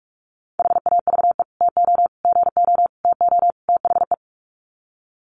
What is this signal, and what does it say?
Morse code, heard as a keyed tone, about 44 wpm, 700 Hz. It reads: HAVE TO GO TO THE
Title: Sets of 5 Words Encore 45wpm